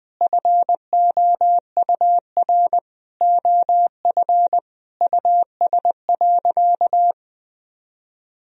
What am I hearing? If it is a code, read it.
Morse code, heard as a keyed tone, about 21 wpm, 700 Hz. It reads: FOUR OF US.
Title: Sentences from Top 1000 Words - Review Part 1 - 20wpm